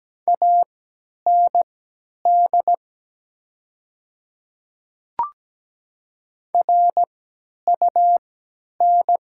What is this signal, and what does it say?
Morse code, heard as a keyed tone, about 17 wpm, spaced as for 10 wpm, 700 Hz. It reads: AND RUN
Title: 3x Character Spacing - Top 500 Words - 17wpm